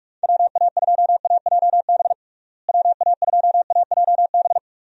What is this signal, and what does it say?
Morse code, heard as a keyed tone, about 42 wpm, 700 Hz. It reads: WA2AJB WA2AJB
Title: USA Call Signs - 6 Characters - Repeated 3x - 45wpm